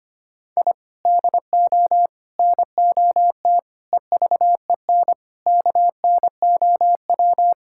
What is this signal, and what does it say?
Morse code, heard as a keyed tone, about 25 wpm, 700 Hz. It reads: I DO NOT EVEN KNOW
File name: practice / 25wpm / Sentences with 4-Word Phrases - 25wpm